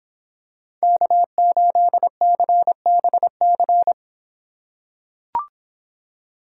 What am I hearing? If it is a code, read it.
Morse code, heard as a keyed tone, about 26 wpm, 700 Hz. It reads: K8CBC